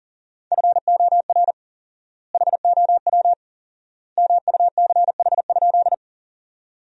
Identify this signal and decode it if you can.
Morse code, heard as a keyed tone, about 37 wpm, 700 Hz. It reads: FOR HOW MUCH?
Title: 4x Word-Spacing - Sentences from Top 200 Words - 40wpm